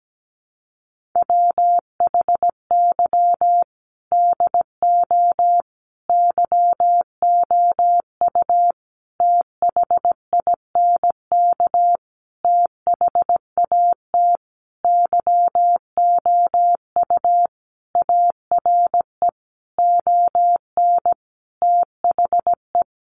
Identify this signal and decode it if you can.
Morse code, heard as a keyed tone, about 17 wpm, 700 Hz. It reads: WHY DO YOU THINK THAT YOU ARE ON THE
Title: Sentences from Top 1400 Words 17wpm